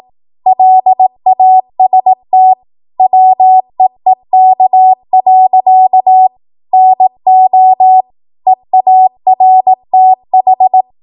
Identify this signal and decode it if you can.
Morse code, heard as a keyed tone, about 18 wpm, 750 Hz. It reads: LAST WEEK. NO EARTH
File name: Morse / 18WPM